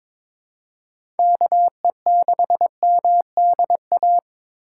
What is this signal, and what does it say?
Morse code, heard as a keyed tone, about 22 wpm, 700 Hz. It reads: KE6MDA
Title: USA Call Signs - 6 Characters - 4x Character Spacing - 22wpm